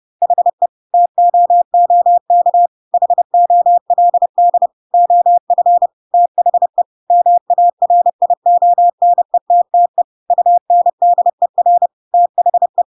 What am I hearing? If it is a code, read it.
Morse code, heard as a keyed tone, about 30 wpm, 700 Hz. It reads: HE TOOK HOLD OF THE MARIONETTE UNDER THE
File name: Pinocchio / Pinocchio ch3 30 30 ik1hge